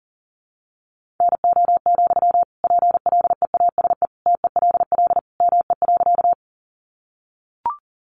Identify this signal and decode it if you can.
Morse code, heard as a keyed tone, about 40 wpm, 700 Hz. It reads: NO, PLEASE TELL ME.
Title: Sentences from Top 1500 Words 40wpm